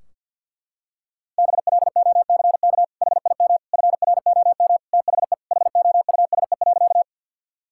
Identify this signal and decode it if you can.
Morse code, heard as a keyed tone, about 50 wpm, 700 Hz. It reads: BLOCK HIM FROM THE HOUSE.
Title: Sentences from Top 1600 Words 50wpm